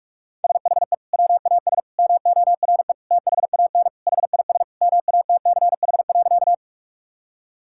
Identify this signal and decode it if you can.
Morse code, heard as a keyed tone, about 41 wpm, 700 Hz. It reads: SHE WAS MORE THAN HIS MATCH.
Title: Sentences from Top 1600 Words 45wpm